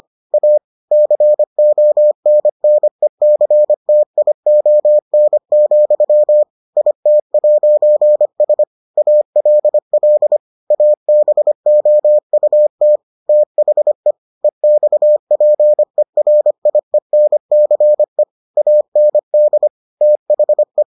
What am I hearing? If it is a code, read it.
Morse code, heard as a keyed tone, about 25 wpm, 600 Hz. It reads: A CONNECTION, IT'S ALL ABOUT THE EXPERIENCE AND THE